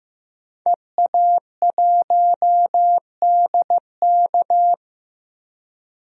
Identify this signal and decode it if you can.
Morse code, heard as a keyed tone, about 15 wpm, 700 Hz. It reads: EA1DK
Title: DXCC-100 Call Signs - 5 Characters - 15wpm